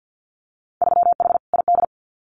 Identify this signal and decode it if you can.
Morse code, heard as a keyed tone, about 50 wpm, 700 Hz. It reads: 3H ID
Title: Winter Field Day Exchanges Encore 50wpm